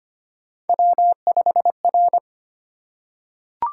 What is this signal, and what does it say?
Morse code, heard as a keyed tone, about 25 wpm, 700 Hz. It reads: W5R
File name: Three Letter-Number Combinations - 25wpm